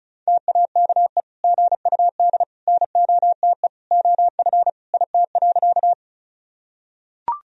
Morse code, heard as a keyed tone, about 34 wpm, 700 Hz. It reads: TAKE GUD NOTE OF IT.
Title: Sentences from Top 500 Words and Abbreviations 35wpm